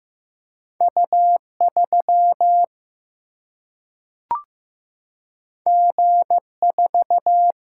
Morse code, heard as a keyed tone, about 15 wpm, 700 Hz. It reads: U 3 G 4
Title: Single-Letter Single-Number 15wpm